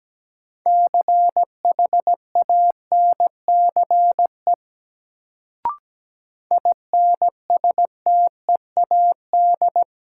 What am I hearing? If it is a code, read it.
Morse code, heard as a keyed tone, about 17 wpm, 700 Hz. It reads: CHANCE INSTEAD
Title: Sentences from Top 800 Words - New Word Drill - 17wpm